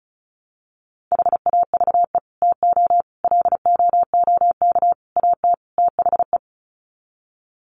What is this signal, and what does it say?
Morse code, heard as a keyed tone, about 34 wpm, 700 Hz. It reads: HAVE TO LOOK AT THE
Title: Sets of 5 Words Encore 35wpm